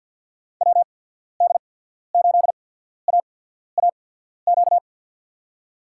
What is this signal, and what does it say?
Morse code, heard as a keyed tone, about 53 wpm, spaced as for 14 wpm, 690 Hz. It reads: WD8AAQ
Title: USA Call Signs - 6 Characters - 8x Character Spacing - 50wpm